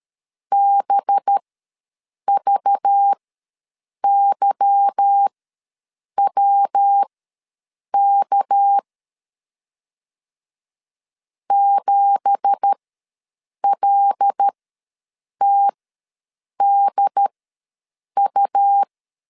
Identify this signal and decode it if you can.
Morse code, heard as a keyed tone, about 13 wpm, spaced as for 7 wpm, 800 Hz. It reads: BVYWK 7LTDU
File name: LEZIONI / 9 2867 ws 12wpm 3431